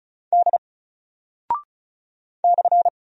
Morse code, heard as a keyed tone, about 36 wpm, 700 Hz. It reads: D /
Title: Single Character (Letter:Number:Punctuation:Prosign) 35wpm